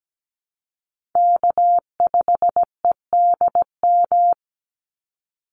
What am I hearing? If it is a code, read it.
Morse code, heard as a keyed tone, about 17 wpm, 700 Hz. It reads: K5EDM
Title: Call Signs - 17wpm